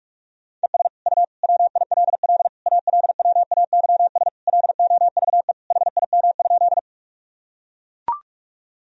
Morse code, heard as a keyed tone, about 41 wpm, 700 Hz. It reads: ES U WILL ALWAYS LOVE HIM?
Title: Sentences from Top 400 Words and Abbreviations 45wpm